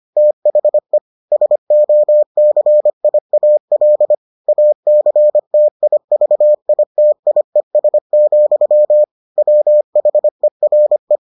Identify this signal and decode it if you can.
Morse code, heard as a keyed tone, about 25 wpm, 600 Hz. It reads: THE SOCIAL ACTIVITIES, WHERE